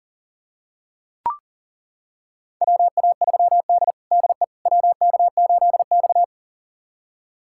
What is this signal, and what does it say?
Morse code, heard as a keyed tone, about 40 wpm, 700 Hz. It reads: WA3D DE WK8X
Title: USA Callsign DE Callsign - 4 Characters - 40wpm